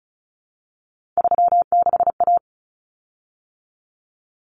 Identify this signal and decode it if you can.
Morse code, heard as a keyed tone, about 34 wpm, 700 Hz. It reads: 36A
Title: Common Cut Numbers - 3-Digit 35wpm